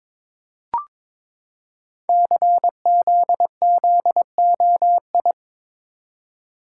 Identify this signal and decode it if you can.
Morse code, heard as a keyed tone, about 22 wpm, 700 Hz. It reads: CZZOI
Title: Code Groups - 5 Letters - 22wpm